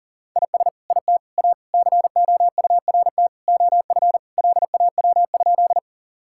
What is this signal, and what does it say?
Morse code, heard as a keyed tone, about 40 wpm, 700 Hz. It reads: IS IT A COURT OF LAW?
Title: Sentences from Top 1600 Words 40wpm